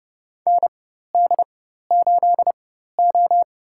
Morse code, heard as a keyed tone, about 30 wpm, spaced as for 14 wpm, 700 Hz. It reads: ND8O
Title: USA Call Signs - 4 Characters - 4x Character Spacing - 30wpm